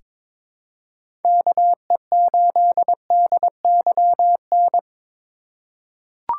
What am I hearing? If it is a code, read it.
Morse code, heard as a keyed tone, about 21 wpm, 700 Hz. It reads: KE8DYN
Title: USA Call Signs - 6 Characters - 2x Character Spacing - 22wpm